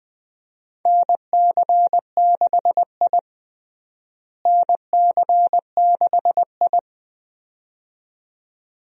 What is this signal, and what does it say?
Morse code, heard as a keyed tone, about 20 wpm, 700 Hz. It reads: NC6I NC6I
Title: USA Call Signs - 4 Characters - Repeated 2x - 20wpm